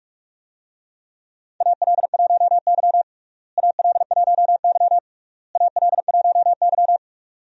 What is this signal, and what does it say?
Morse code, heard as a keyed tone, about 46 wpm, 700 Hz. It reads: AL1Y AL1Y AL1Y
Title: USA Call Signs - 4 Characters - Repeated 3x - 45wpm